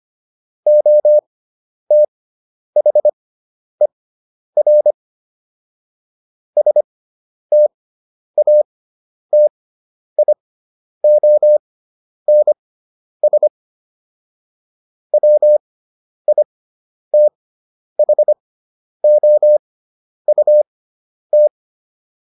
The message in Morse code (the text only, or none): OTHER STATIONS WITHOUT